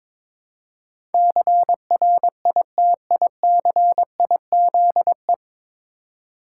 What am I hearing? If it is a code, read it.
Morse code, heard as a keyed tone, about 22 wpm, 700 Hz. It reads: CRITICIZE
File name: 9 Letter Words - First 3 Letters Given - 22wpm